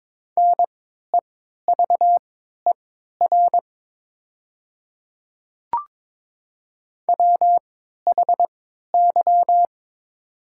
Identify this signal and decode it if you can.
Morse code, heard as a keyed tone, about 22 wpm, spaced as for 13 wpm, 700 Hz. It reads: NEVER WHY
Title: 3x Character Spacing - Top 300 Words - 22wpm